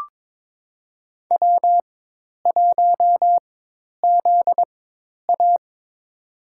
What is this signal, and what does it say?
Morse code, heard as a keyed tone, about 21 wpm, spaced as for 10 wpm, 700 Hz. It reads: W1ZA
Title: USA Call Signs - 4x Character Spacing - 22wpm